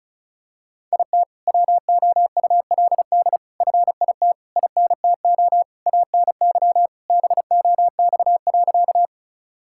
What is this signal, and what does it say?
Morse code, heard as a keyed tone, about 36 wpm, 700 Hz. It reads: IT WOULD FIT INTO ANY BOX.
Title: Sentences from Top 1000 Words - Review Part 5 - 35wpm